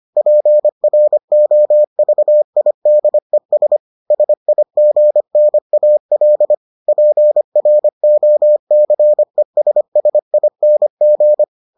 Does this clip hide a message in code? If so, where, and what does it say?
Yes: Morse code from 0.2 to 11.4 s, heard as a keyed tone, about 25 wpm, 600 Hz. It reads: PROVIDES SIGNAL PROCESSING